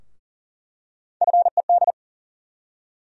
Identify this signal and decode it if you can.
Morse code, heard as a keyed tone, about 40 wpm, 700 Hz. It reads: FED
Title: USA Call Sign Postfixes - 3 Letters 40wpm